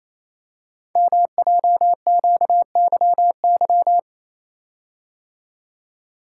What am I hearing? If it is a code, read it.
Morse code, heard as a keyed tone, about 28 wpm, 700 Hz. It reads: MJQYY
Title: Code Groups - 5 Letters - 28wpm